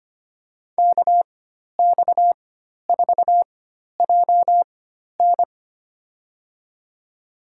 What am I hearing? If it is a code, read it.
Morse code, heard as a keyed tone, about 25 wpm, spaced as for 12 wpm, 700 Hz. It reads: KX4JN
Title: USA Call Signs - 5 Characters - 4x Character Spacing - 25wpm